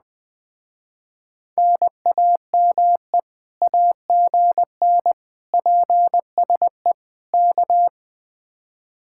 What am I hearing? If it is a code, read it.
Morse code, heard as a keyed tone, about 20 wpm, 700 Hz. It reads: NAME AGN PSE K
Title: Speed-Racing Common QSO Phrases - 17wpm